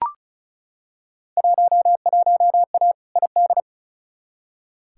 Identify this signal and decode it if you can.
Morse code, heard as a keyed tone, about 36 wpm, 700 Hz. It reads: 11A ID
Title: Field Day Exchanges Encore - 35wpm